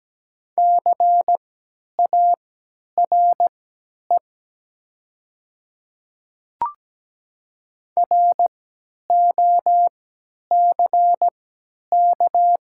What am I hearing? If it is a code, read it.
Morse code, heard as a keyed tone, about 17 wpm, spaced as for 10 wpm, 700 Hz. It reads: CARE ROCK
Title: 3x Character Spacing - Top 500 Words - 17wpm